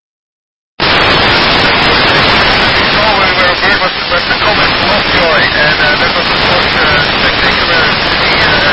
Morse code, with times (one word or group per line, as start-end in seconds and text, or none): none